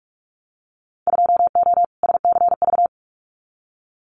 Morse code, heard as a keyed tone, about 46 wpm, 700 Hz. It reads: 2O SCV
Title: Winter Field Day Exchanges Encore 45wpm